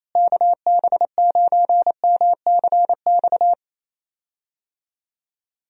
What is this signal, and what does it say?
Morse code, heard as a keyed tone, about 28 wpm, 700 Hz. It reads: KB9MCX
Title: USA Call Signs - 6 Characters - Repeated 3x - 28wpm